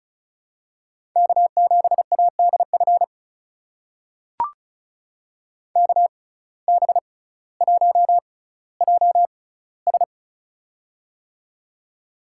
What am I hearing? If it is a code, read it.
Morse code, heard as a keyed tone, about 36 wpm, spaced as for 18 wpm, 700 Hz. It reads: K7ADF KB1JS
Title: USA Call Signs - 5 Characters - 6x Character Spacing - 35wpm